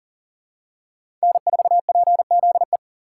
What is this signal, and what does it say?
Morse code, heard as a keyed tone, about 40 wpm, 690 Hz. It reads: N4PZE
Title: USA Call Signs - 4x Character Spacing - 40wpm